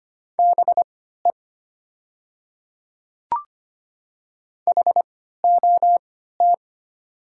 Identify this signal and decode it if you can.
Morse code, heard as a keyed tone, about 25 wpm, spaced as for 14 wpm, 700 Hz. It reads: BE HOT
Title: 3x Character Spacing - Top 100 Words - 25wpm